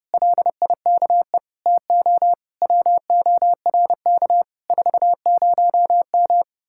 Morse code, heard as a keyed tone, about 33 wpm, 700 Hz. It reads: LIKE TO WORK 40M